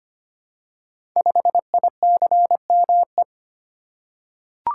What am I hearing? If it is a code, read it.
Morse code, heard as a keyed tone, about 25 wpm, 700 Hz. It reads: 5ICME
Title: Code Groups - 5 Letter:Numbers - 25wpm